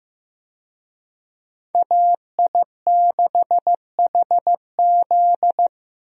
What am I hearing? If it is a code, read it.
Morse code, heard as a keyed tone, about 15 wpm, 700 Hz. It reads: AI6HZ
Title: USA Call Signs - 5 Characters - Repeated 3x - 15wpm